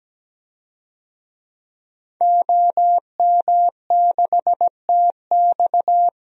Morse code, heard as a keyed tone, about 17 wpm, 700 Hz. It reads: OM6TX
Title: DXCC-100 Call Signs - 5 Characters - 17wpm